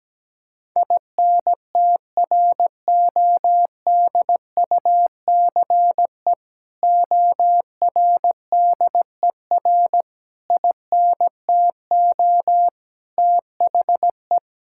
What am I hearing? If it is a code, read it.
Morse code, heard as a keyed tone, about 17 wpm, 700 Hz. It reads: INTRODUCE ORDER INTO THE
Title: Sentences from Top 1600 Words 17wpm